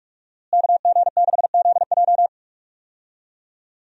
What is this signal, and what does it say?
Morse code, heard as a keyed tone, about 46 wpm, 700 Hz. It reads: KG6ZJ
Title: USA Call Signs - 5 Characters - Repeated 3x - 45wpm